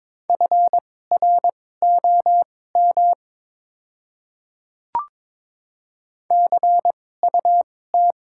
Morse code, heard as a keyed tone, about 22 wpm, spaced as for 16 wpm, 700 Hz. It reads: FROM CUT
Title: 2x Character Spacing - Top 300 Words - 22wpm